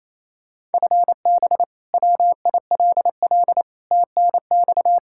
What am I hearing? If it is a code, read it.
Morse code, heard as a keyed tone, about 28 wpm, 700 Hz. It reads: FB WILL TNX
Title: Practice QSO - Exchange 3 - More Info - 28wpm